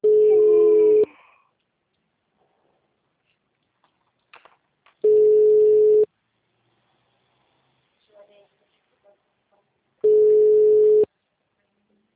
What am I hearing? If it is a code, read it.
Ringback tone (Europe, 425 Hz), with speech.
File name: out- -2016- - - .27762